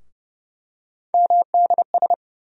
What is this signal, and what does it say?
Morse code, heard as a keyed tone, about 30 wpm, 700 Hz. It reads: MDS
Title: USA Call Sign Postfixes - 3 Letters 30wpm